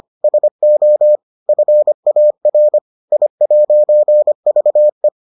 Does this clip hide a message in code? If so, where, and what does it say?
Yes: Morse code from 0.2 to 5.1 s, heard as a keyed tone, about 25 wpm, 600 Hz. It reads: SO FAR I'VE